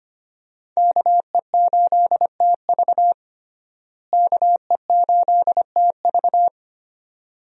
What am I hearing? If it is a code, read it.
Morse code, heard as a keyed tone, about 25 wpm, 700 Hz. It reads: KE8TV KE8TV
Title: USA Call Signs - 5 Characters - Repeated 2x - 25wpm